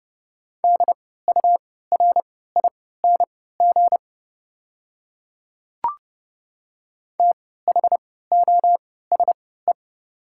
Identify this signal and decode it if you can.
Morse code, heard as a keyed tone, about 32 wpm, spaced as for 17 wpm, 690 Hz. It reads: DURING THOSE
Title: 3x Character Spacing - Top 500 Words - 30wpm